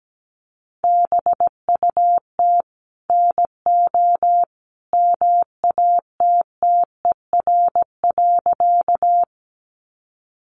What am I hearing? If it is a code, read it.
Morse code, heard as a keyed tone, about 17 wpm, 700 Hz. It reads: BUT NO MATTER.
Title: Sentences from Top 700 Words 17wpm